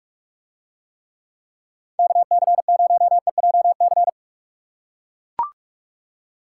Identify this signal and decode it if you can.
Morse code, heard as a keyed tone, about 44 wpm, 700 Hz. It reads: KC0EJC